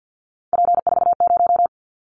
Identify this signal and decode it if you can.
Morse code, heard as a keyed tone, about 50 wpm, 690 Hz. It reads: P40
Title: Callsign Prefixes for Top 100 DXCC - 50wpm